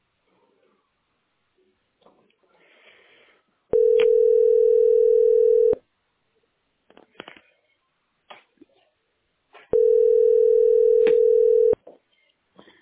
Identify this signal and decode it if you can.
Ringback tone (North America), 440 + 480 Hz.
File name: 3feaad58-6da4-4a96-a490